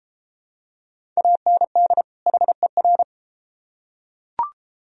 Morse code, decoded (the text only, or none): AND HER